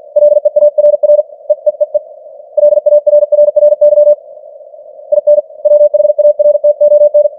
Morse code, heard as a keyed tone, about 46 wpm, 590 Hz. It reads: LEARR EEEE LARRRY IN WHARTON
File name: archived / CWBros